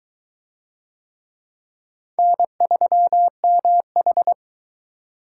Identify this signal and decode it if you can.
Morse code, heard as a keyed tone, about 19 wpm, 700 Hz. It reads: N3MH